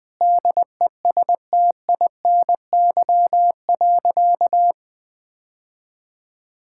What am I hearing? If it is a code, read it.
Morse code, heard as a keyed tone, about 20 wpm, 700 Hz. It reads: DESTINY.
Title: Sentences with 6-Word Phrases - Spoken Beginning - 20wpm